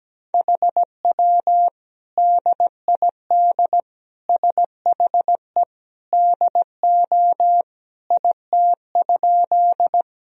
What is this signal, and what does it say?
Morse code, heard as a keyed tone, about 17 wpm, 700 Hz. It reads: HW DID SHE DO IT?
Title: Sentences from Top 100 Words and Abbreviations 17wpm